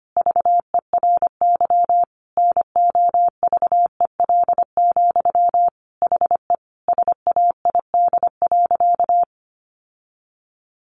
Morse code, heard as a keyed tone, about 25 wpm, 700 Hz. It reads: VERY NOVEL, HE SAID.
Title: Sentences from Top 1300 Words 25wpm